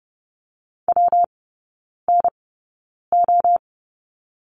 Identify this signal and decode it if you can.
Morse code, heard as a keyed tone, about 30 wpm, spaced as for 9 wpm, 700 Hz. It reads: WNO
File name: Lesson 19e - Letter w - Three Characters 30wpm